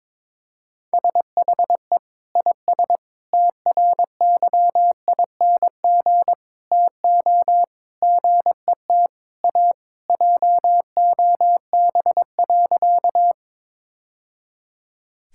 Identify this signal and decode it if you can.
Morse code, heard as a keyed tone, about 22 wpm, 700 Hz. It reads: SHE IS TRYING TO GET A JOB.
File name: practice / 22wpm / Sentences with 3-Word Phrases - 22wpm